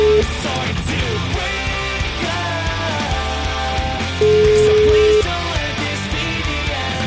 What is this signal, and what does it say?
Ringback tone (Europe, 425 Hz), with noise.